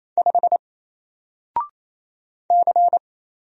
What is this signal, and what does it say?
Morse code, heard as a keyed tone, about 27 wpm, 700 Hz. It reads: 5 C
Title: Single Letter-Number Alt 28wpm